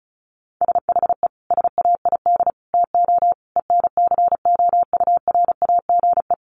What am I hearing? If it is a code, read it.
Morse code, heard as a keyed tone, about 34 wpm, 700 Hz. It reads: SHE SAID TO ENCOURAGE
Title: Sentences from Top 1400 Words 35wpm